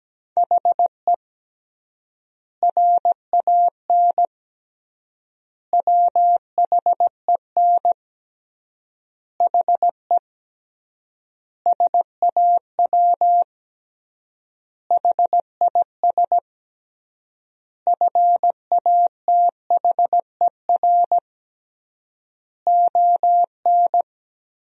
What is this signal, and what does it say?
Morse code, heard as a keyed tone, about 17 wpm, 700 Hz. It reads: HE RAN WHEN HE SAW HIS FATHER ON
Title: 3x Word-Spacing - Sentences from Top 500 Words - 17wpm